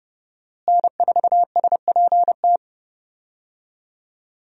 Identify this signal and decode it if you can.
Morse code, heard as a keyed tone, about 30 wpm, 700 Hz. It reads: N4SPT
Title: USA Call Signs - 5 Characters - Repeated 2x - 30wpm